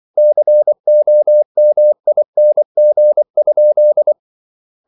Morse code, heard as a keyed tone, about 24 wpm, 600 Hz. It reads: COMING?